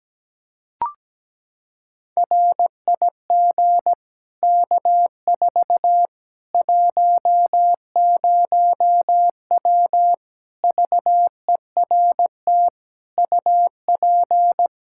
Morse code, heard as a keyed tone, about 17 wpm, 700 Hz. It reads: RIG K4 10W VERT UP
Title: Practice QSO - Exchange 2 - Info - 17wpm